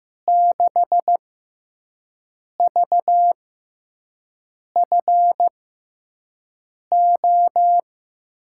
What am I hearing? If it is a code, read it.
Morse code, heard as a keyed tone, about 15 wpm, spaced as for 5 wpm, 700 Hz. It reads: 6VFO
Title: USA Call Signs - 5 Characters - 6x Character Spacing - 15wpm